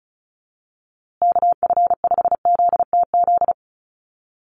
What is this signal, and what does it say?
Morse code, heard as a keyed tone, about 34 wpm, 700 Hz. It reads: KF5ZTZ
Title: USA Call Signs Repeated 3x 35wpm